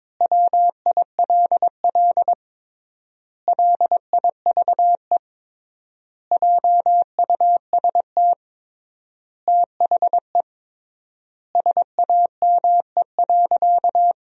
Morse code, heard as a keyed tone, about 22 wpm, 700 Hz. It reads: WILL LIVE JUST THE SAME.
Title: 3x Word-Spacing - Sentences from Top 300 Words - 22wpm